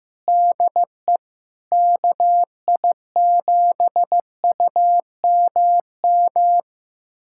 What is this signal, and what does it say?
Morse code, heard as a keyed tone, about 15 wpm, 700 Hz. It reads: DE KI7UMM
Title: USA Callsign DE Callsign 15wpm